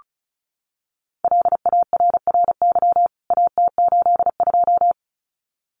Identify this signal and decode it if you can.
Morse code, heard as a keyed tone, about 34 wpm, 700 Hz. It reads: LARRY AT82
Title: CWT Exchange 35wpm